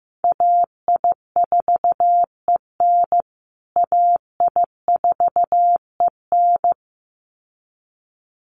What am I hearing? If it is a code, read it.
Morse code, heard as a keyed tone, about 15 wpm, 700 Hz. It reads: AI4EN AI4EN
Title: USA Call Signs Repeated 2x 15wpm